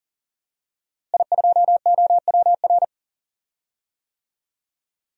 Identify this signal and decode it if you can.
Morse code, heard as a keyed tone, about 40 wpm, 700 Hz. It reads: I2OWR
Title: DXCC-100 Call Signs - 5 Characters - Repeated 2x - 40wpm